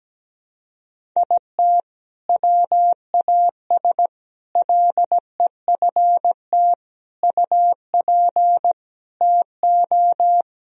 Morse code, heard as a keyed tone, about 17 wpm, 700 Hz. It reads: IT WAS LEFT UP TO
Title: Sentences from Top 1000 Words - Review Part 4 - 17wpm